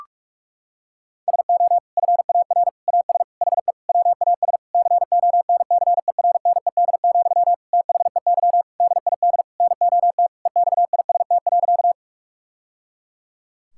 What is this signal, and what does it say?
Morse code, heard as a keyed tone, about 46 wpm, 700 Hz. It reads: SO FAR AS HE WAS CONCERNED, THEY DID NOT EXIST.
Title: Sentences from Top 1000 Words - Review Part 6 - 45wpm